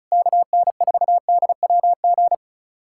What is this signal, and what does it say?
Morse code, heard as a keyed tone, about 34 wpm, 700 Hz. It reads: KN4DWG
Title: USA Callsign DE Callsign 35wpm